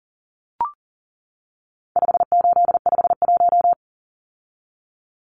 Morse code, heard as a keyed tone, about 40 wpm, 700 Hz. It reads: 5851
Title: Numbers - 4-Digit 40wpm